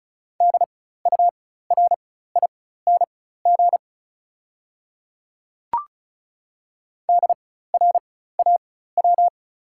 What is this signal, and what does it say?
Morse code, heard as a keyed tone, about 36 wpm, spaced as for 16 wpm, 690 Hz. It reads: DURING DRAW
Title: 4x Character Spacing - Top 500 Words - 35wpm